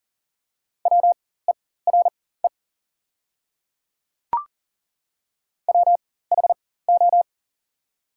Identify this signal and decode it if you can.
Morse code, heard as a keyed tone, about 40 wpm, spaced as for 19 wpm, 690 Hz. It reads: WERE WHO
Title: 4x Character Spacing - Top 100 Words - 40wpm